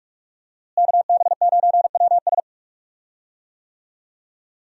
Morse code, heard as a keyed tone, about 41 wpm, 700 Hz. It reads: KB9WS
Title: USA Call Signs - 5 Characters - Repeated 2x - 45wpm